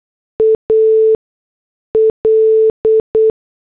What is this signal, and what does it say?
Morse code, heard as a keyed tone, about 8 wpm, spaced as for 6 wpm, 440 Hz. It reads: AL
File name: morse / headlines